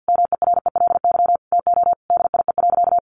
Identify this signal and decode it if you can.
Morse code, heard as a keyed tone, about 50 wpm, 700 Hz. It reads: MERELY TO DIE.
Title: Sentences from Top 1400 Words 50wpm